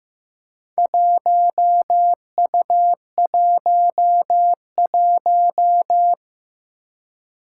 Morse code, heard as a keyed tone, about 15 wpm, 700 Hz. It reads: 1U11
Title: Four Challenging Letter-Number Combinations 15wpm